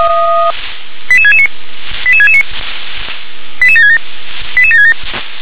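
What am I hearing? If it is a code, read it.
Five-tone selective call (ZVEI), heard as quick runs of five tones, tones of about 70 ms. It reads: 88669 88669 88767 88767
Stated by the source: ZVEI five-tone call 88556, received by SDR.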